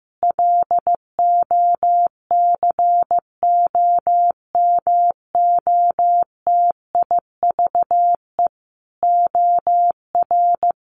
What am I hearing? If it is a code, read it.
Morse code, heard as a keyed tone, about 15 wpm, 700 Hz. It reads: LOCOMOTIVE OR
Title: A Sound Of Thunder 15wpm